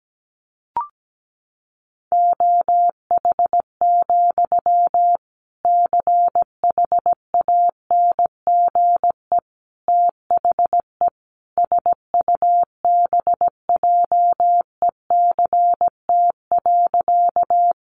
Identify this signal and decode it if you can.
Morse code, heard as a keyed tone, about 17 wpm, 700 Hz. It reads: OH, CHANGE THE SUBJECT.
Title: Sentences from Top 800 Words 17wpm